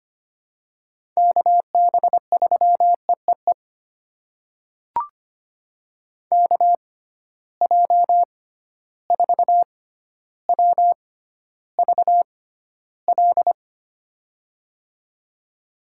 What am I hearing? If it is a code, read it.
Morse code, heard as a keyed tone, about 25 wpm, spaced as for 13 wpm, 700 Hz. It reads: KB3EEE KJ4WVL
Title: USA Call Signs - 6x Character Spacing - 25wpm